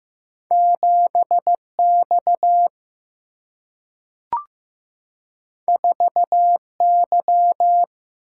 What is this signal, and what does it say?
Morse code, heard as a keyed tone, about 15 wpm, 700 Hz. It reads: 7X 4Y
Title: Single-Letter Single-Number 15wpm